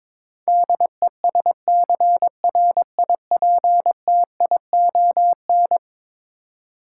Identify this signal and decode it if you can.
Morse code, heard as a keyed tone, about 23 wpm, 700 Hz. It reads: DESCRIPTION
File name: practice / 22wpm / Eleven Letter Words - 22wpm